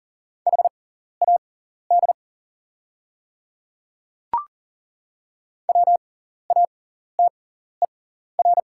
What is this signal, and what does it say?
Morse code, heard as a keyed tone, about 40 wpm, spaced as for 14 wpm, 690 Hz. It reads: HAD WATER
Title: 6x Character Spacing - Top 500 Words - 40wpm